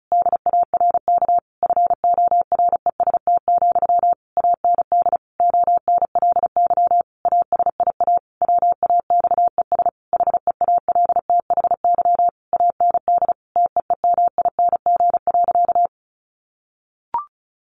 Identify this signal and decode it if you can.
Morse code, heard as a keyed tone, about 34 wpm, 700 Hz. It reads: DARK FOREST, AND ONLY ASIA WAXES HEALTHY AND TEEMING.
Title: A Sound Of Thunder 35wpm